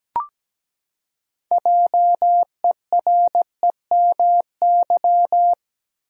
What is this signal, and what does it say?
Morse code, heard as a keyed tone, about 17 wpm, 700 Hz. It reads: JEREMY
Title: USA Common Names - 17wpm